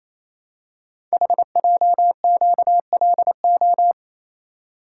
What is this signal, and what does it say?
Morse code, heard as a keyed tone, about 27 wpm, 700 Hz. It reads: HJQLO
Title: Code Groups - 5 Letters - 28wpm